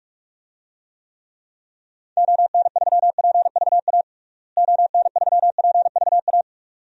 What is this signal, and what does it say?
Morse code, heard as a keyed tone, about 43 wpm, 700 Hz. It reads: ON3PVA ON3PVA
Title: DXCC-100 Call Signs - 6 Characters - Repeated 2x - 45wpm